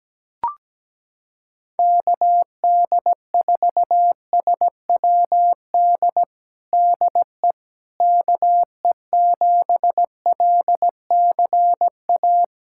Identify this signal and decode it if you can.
Morse code, heard as a keyed tone, about 17 wpm, 700 Hz. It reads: KD4SWD DE KE7LCA
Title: USA Callsign DE Callsign - 6 Characters - 17wpm